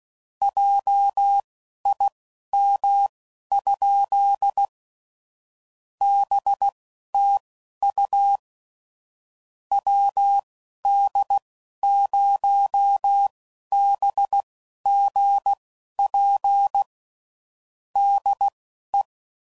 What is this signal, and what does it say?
Morse code, heard as a keyed tone, about 16 wpm, spaced as for 11 wpm, 800 Hz. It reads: JIM? BTU WD0BGP DE
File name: CW Code Groups @15WPM / QSOs / WD0BGP Spaced